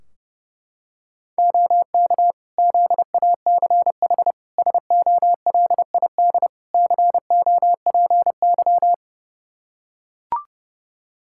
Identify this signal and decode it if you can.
Morse code, heard as a keyed tone, about 30 wpm, 700 Hz. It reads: OK ZACH SOLID COPY
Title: K1USN SST QSO Practice 30wpm